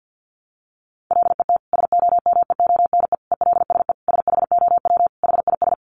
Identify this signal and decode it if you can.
Morse code, heard as a keyed tone, about 50 wpm, 700 Hz. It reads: LET SOMEONE ELSE SHOW HIS
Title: Sentences from Top 1200 Words 50wpm